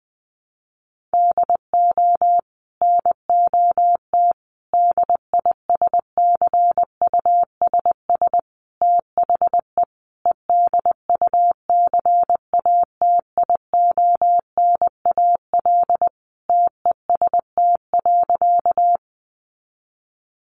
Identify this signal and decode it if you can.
Morse code, heard as a keyed tone, about 20 wpm, 700 Hz. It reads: DO NOT DISCUSS THE EDUCATIONAL TEST.
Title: Sentences from Top 1200 Words 20wpm